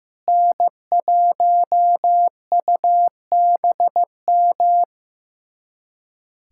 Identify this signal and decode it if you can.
Morse code, heard as a keyed tone, about 15 wpm, 700 Hz. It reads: N1UBM
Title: USA Callsign DE Callsign - 5 Characters - 15wpm